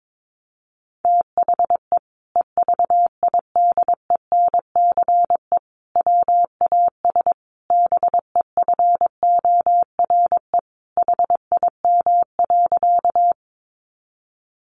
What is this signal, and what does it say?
Morse code, heard as a keyed tone, about 22 wpm, 700 Hz. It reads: THE EVIDENCE WAS BEFORE HIM.
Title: Sentences from Top 1300 Words 22wpm